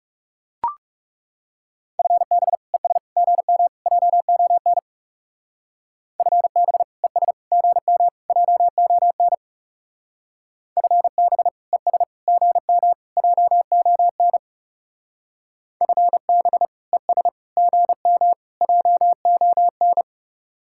Morse code, heard as a keyed tone, about 38 wpm, 700 Hz. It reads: FB ES GM JON FB ES GM JON FB ES GM JON FB ES GM JON
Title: Speed-Racing Common QSO Phrases Encore - 30wpm